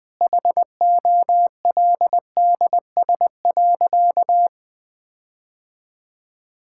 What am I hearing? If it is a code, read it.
Morse code, heard as a keyed tone, about 20 wpm, 700 Hz. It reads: HOLDS.
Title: Sentences with 5-Word Phrases - Spoken Beginning - 20wpm